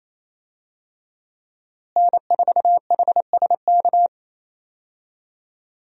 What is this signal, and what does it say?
Morse code, heard as a keyed tone, about 28 wpm, 700 Hz. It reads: N4HSK
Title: USA Call Signs - 5 Characters - Repeated 3x - 28wpm